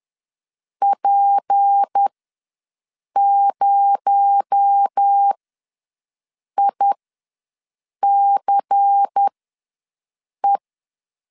Morse code, heard as a keyed tone, about 10 wpm, spaced as for 6 wpm, 800 Hz. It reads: P0ICE
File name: LEZIONI / 6 PKX0 / 6 PKX0 ws 10wpm 4220